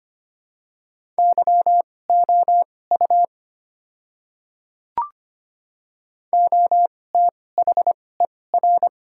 Morse code, heard as a keyed tone, about 25 wpm, spaced as for 18 wpm, 700 Hz. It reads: YOU OTHER
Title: 2x Character Spacing - Top 100 Words - 25wpm